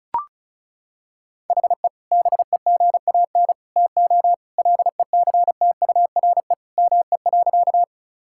Morse code, heard as a keyed tone, about 36 wpm, 700 Hz. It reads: HE BEGAN TO LECTURE ME.
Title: Sentences from Top 1600 Words 35wpm